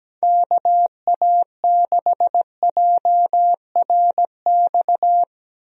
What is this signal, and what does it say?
Morse code, heard as a keyed tone, about 17 wpm, 700 Hz. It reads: KA6JRX
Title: USA Call Signs - 6 Characters - Repeated 3x - 17wpm